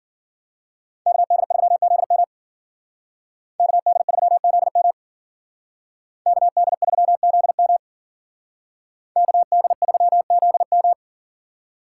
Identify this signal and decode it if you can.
Morse code, heard as a keyed tone, about 49 wpm, 700 Hz. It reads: KD3ZM KD3ZM KD3ZM KD3ZM
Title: Speed Racing - USA 2x2 Call Signs 40wpm